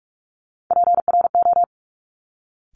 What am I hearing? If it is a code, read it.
Morse code, heard as a keyed tone, about 46 wpm, 700 Hz. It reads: PRO
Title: Top Three Letter Combinations in Words 45wpm